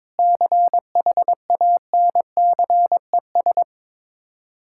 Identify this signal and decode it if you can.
Morse code, heard as a keyed tone, about 21 wpm, 700 Hz. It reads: CHANCES
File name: practice / 22wpm / Sets of 8 Words - 22wpm